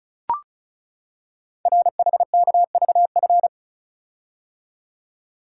Morse code, heard as a keyed tone, about 36 wpm, 700 Hz. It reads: RHKVF
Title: Code Groups - 5 Letters - 35wpm